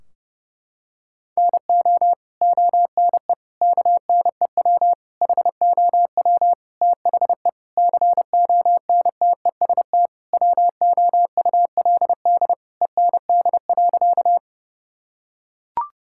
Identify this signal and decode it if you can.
Morse code, heard as a keyed tone, about 29 wpm, 700 Hz. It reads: NO ONE KNEW HOW THE CONTEST WOULD END.
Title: Sentences from Top 1600 Words 30wpm